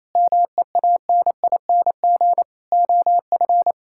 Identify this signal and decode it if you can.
Morse code, heard as a keyed tone, about 28 wpm, 700 Hz. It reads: MEANING OF
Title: Sentences from Top 1600 Words 28wpm